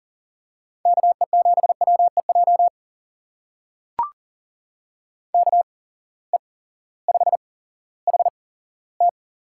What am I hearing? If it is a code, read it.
Morse code, heard as a keyed tone, about 40 wpm, spaced as for 18 wpm, 700 Hz. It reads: KE7WEJ KE5HT
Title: USA Call Signs - 6 Characters - 8x Character Spacing - 40wpm